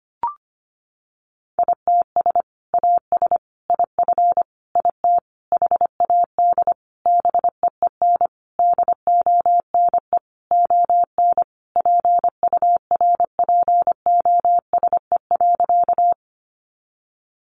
Morse code, heard as a keyed tone, about 25 wpm, 700 Hz. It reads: ITS AS IF IT HAD BEEN DONE ON PURPOSE.
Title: Sentences from Top 800 Words 25wpm